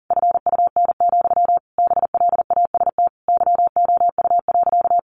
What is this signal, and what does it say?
Morse code, heard as a keyed tone, about 40 wpm, 700 Hz. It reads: FUN, BLAST YOU.
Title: A Sound Of Thunder 40wpm